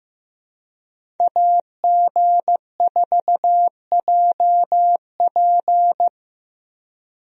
Morse code, heard as a keyed tone, about 15 wpm, 700 Hz. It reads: AG4JP
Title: USA Call Signs - 5 Characters - 15wpm